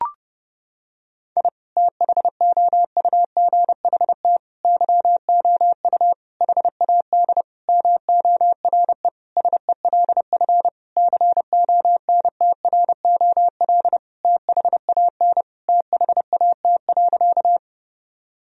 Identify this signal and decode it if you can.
Morse code, heard as a keyed tone, about 32 wpm, 700 Hz. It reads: I THOUGHT YOU HAD MORE SELF CONTROL THAN THAT.
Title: Sentences from Top 1000 Words - Review Part 3 - 30wpm